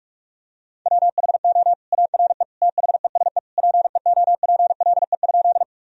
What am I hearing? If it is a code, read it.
Morse code, heard as a keyed tone, about 41 wpm, 700 Hz. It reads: WHO ARE THESE PEOPLE?
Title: Sentences from Top 1000 Words - Review Part 1 - 45wpm